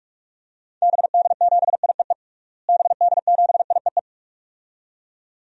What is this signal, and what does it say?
Morse code, heard as a keyed tone, about 46 wpm, 700 Hz. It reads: BD7IEE BD7IEE
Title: DXCC-100 Call Signs - 6 Characters - Repeated 2x - 45wpm